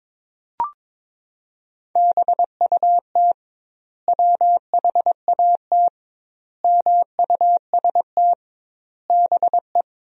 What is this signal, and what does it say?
Morse code, heard as a keyed tone, about 22 wpm, 700 Hz. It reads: BUT WHAT MUST BE
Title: 2x Word-Spacing - Sentences from Top 500 Words - 22wpm